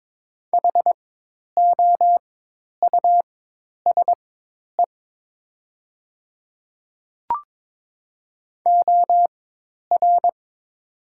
Morse code, heard as a keyed tone, about 22 wpm, spaced as for 10 wpm, 700 Hz. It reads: HOUSE OR
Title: 4x Character Spacing - Top 300 Words - 22wpm